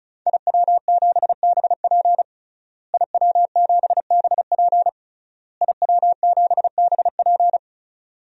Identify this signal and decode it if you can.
Morse code, heard as a keyed tone, about 34 wpm, 700 Hz. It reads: IW7BP IW7BP IW7BP
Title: DXCC-100 Call Signs - 5 Characters - Repeated 3x - 35wpm